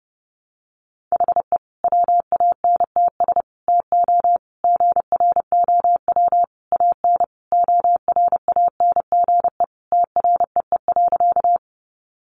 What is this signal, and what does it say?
Morse code, heard as a keyed tone, about 28 wpm, 700 Hz. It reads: HE WANTS TO GROW AN ORANGE TREE.
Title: Easy Phrases 30wpm